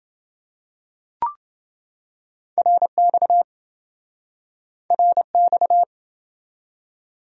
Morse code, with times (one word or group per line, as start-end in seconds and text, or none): 2.6-3.4 RX
4.9-5.8 RX